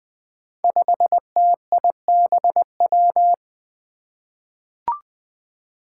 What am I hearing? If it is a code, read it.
Morse code, heard as a keyed tone, about 20 wpm, 700 Hz. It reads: 5TIBW
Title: Five Challenging Letter-Number Combinations 20wpm